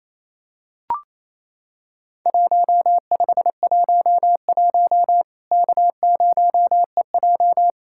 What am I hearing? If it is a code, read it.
Morse code, heard as a keyed tone, about 28 wpm, 700 Hz. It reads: 1511 K0EJ
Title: Worked All Europe DX Contest - QTC Exchange - 28wpm